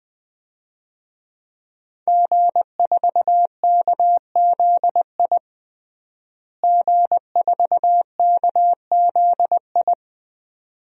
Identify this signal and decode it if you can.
Morse code, heard as a keyed tone, about 20 wpm, 700 Hz. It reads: G4KZI G4KZI
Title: DXCC-100 Call Signs - 5 Characters - Repeated 2x - 20wpm